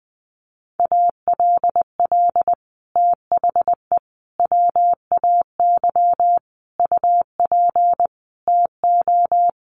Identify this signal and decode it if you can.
Morse code, heard as a keyed tone, about 20 wpm, 700 Hz. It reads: ALL THE WAY UP TO
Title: Sets of 5 Words Encore 20wpm